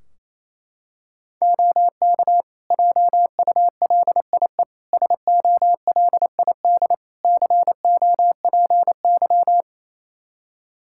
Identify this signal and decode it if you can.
Morse code, heard as a keyed tone, about 28 wpm, 700 Hz. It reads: OK JULIE SOLID COPY
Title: Practice QSO - Exchange 3 - More Info - 28wpm